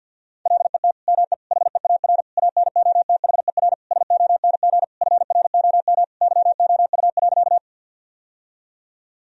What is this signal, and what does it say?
Morse code, heard as a keyed tone, about 50 wpm, 700 Hz. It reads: LET ME HEAR ANOTHER SONG FROM YOU.
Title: Sentences from Top 1000 Words - Review Part 2 - 50wpm